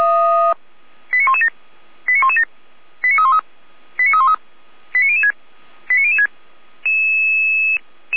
Five-tone selective call (ZVEI), heard as quick runs of five tones, tones of about 70 ms. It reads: 89117 89117 89313 89313 89006 89006